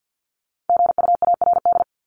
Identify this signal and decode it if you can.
Morse code, heard as a keyed tone, about 50 wpm, 700 Hz. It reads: GUARD
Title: Top 501-750 Words 50wpm